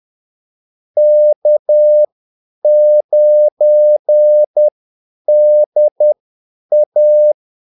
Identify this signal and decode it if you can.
Morse code, heard as a keyed tone, about 10 wpm, spaced as for 8 wpm, 600 Hz. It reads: K9DA
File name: Callsigns-10-8wpm